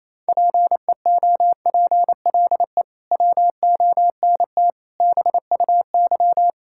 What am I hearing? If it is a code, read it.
Morse code, heard as a keyed tone, about 27 wpm, 700 Hz. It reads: PEOPLE WONT BUY
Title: Sentences from Top 1000 Words - Review Part 6 - 28wpm